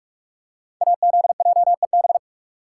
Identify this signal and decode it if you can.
Morse code, heard as a keyed tone, about 44 wpm, 700 Hz. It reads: AZJEB
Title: Code Groups - 5 Letter:Numbers - 45wpm